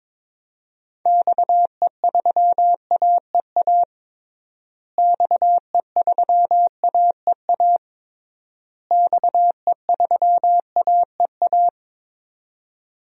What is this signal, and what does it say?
Morse code, heard as a keyed tone, about 22 wpm, 700 Hz. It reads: XE3AEA XE3AEA XE3AEA
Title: DXCC-100 Call Signs - 6 Characters - Repeated 3x - 22wpm